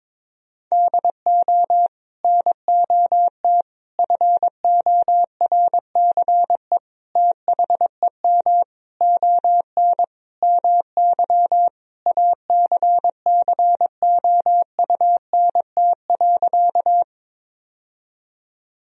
Morse code, heard as a keyed tone, about 22 wpm, 700 Hz. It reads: DO NOT FORCE THEM ON MY ACCOUNT.
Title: Sentences from Top 1000 Words - Review Part 3 - 22wpm